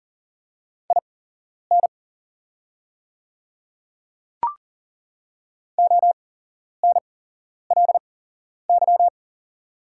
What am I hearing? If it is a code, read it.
Morse code, heard as a keyed tone, about 40 wpm, spaced as for 11 wpm, 700 Hz. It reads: IN ONLY
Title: 8x Character Spacing - Top 200 Words - 40wpm